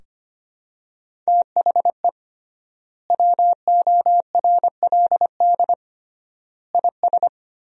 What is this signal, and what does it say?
Morse code, heard as a keyed tone, about 25 wpm, 700 Hz. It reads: THE WORLD IS
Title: 3x Word-Spacing - Sentences from Top 400 Words - 25wpm